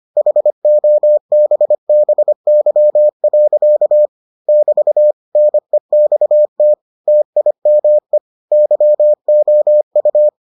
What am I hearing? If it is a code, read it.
Morse code, heard as a keyed tone, about 25 wpm, 600 Hz. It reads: HOBBY. <BT> NEXT TIME YOU